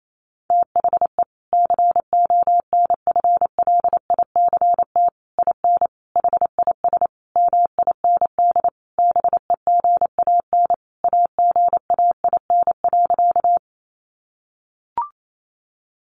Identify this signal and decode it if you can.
Morse code, heard as a keyed tone, about 28 wpm, 700 Hz. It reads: THE CONFLICT IN HIS MIND BEGAN AGAIN.
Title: Sentences from Top 1300 Words 28wpm